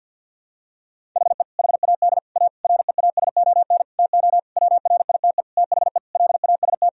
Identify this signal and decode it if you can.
Morse code, heard as a keyed tone, about 51 wpm, 700 Hz. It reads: HE HAD A REASON TO WRITE THE LAST
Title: Sentences from Top 1000 Words - Review Part 1 - 50wpm